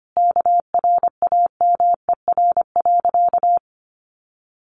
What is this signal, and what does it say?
Morse code, heard as a keyed tone, about 25 wpm, 700 Hz. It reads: KRAMER.
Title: A Sound Of Thunder 25wpm